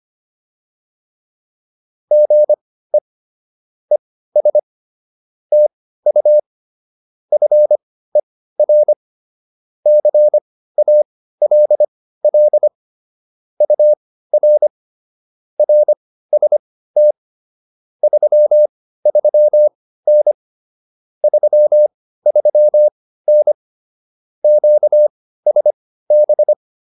Morse code, heard as a keyed tone, about 25 wpm, spaced as for 15 wpm, 600 Hz. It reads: GE ES TU FER CALL UR RST 33N 33N QSB